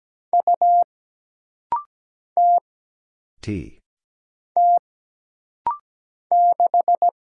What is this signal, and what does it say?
Morse code, heard as a keyed tone, about 17 wpm, 700 Hz. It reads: U T T 6
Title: Single Letter-Number - Rapid-Fire 17wpm